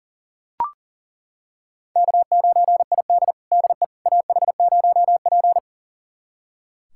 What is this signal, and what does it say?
Morse code, heard as a keyed tone, about 40 wpm, 700 Hz. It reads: K9ID DE AH0P
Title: USA Callsign DE Callsign - 4 Characters - 40wpm